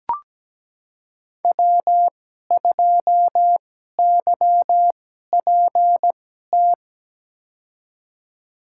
Morse code, heard as a keyed tone, about 17 wpm, spaced as for 12 wpm, 700 Hz. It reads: W2YPT
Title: USA Call Signs - 5 Characters - 2x Character Spacing - 17wpm